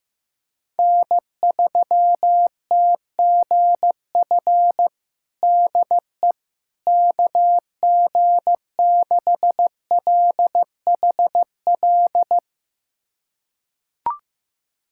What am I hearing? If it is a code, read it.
Morse code, heard as a keyed tone, about 15 wpm, 700 Hz. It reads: N3TGF DE KG6LHL
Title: USA Callsign DE Callsign 15wpm